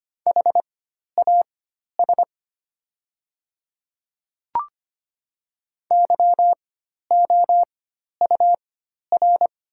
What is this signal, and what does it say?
Morse code, heard as a keyed tone, about 25 wpm, spaced as for 12 wpm, 700 Hz. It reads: HAS YOUR
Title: 4x Character Spacing - Top 100 Words - 25wpm